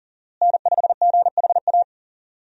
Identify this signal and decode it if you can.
Morse code, heard as a keyed tone, about 39 wpm, 700 Hz. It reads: N5GHA
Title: USA Call Signs - 5 Characters - Repeated 2x - 40wpm